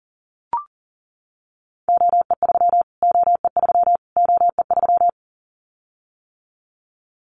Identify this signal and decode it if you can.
Morse code, heard as a keyed tone, about 40 wpm, 700 Hz. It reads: OE3 OE3 OE3
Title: Callsign Prefixes for Top 25 DXCC - Repeated 3x - 40wpm